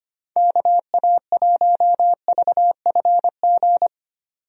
Morse code, heard as a keyed tone, about 25 wpm, 700 Hz. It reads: KA1VFG
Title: USA Call Signs - 6 Characters - 25wpm